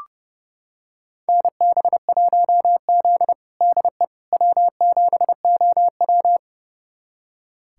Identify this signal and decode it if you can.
Morse code, heard as a keyed tone, about 30 wpm, 700 Hz. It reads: NB1Z DE W7OW
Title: USA Callsign DE Callsign - 4 Characters - 30wpm